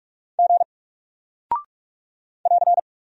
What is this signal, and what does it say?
Morse code, heard as a keyed tone, about 46 wpm, 700 Hz. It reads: G <AR>
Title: Single Character (Letter:Number:Punctuation:Prosign) Alt 45wpm